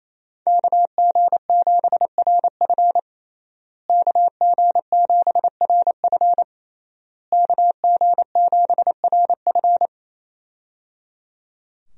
Morse code, heard as a keyed tone, about 27 wpm, 700 Hz. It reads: KG7RF KG7RF KG7RF
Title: USA Call Signs - 5 Characters - Repeated 3x - 28wpm